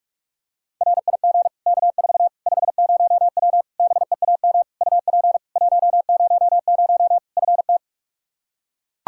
Morse code, heard as a keyed tone, about 41 wpm, 700 Hz. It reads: RIG K4 50W BEAM UP 100 FT
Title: Practice QSO - Exchange 2 - Info - 45wpm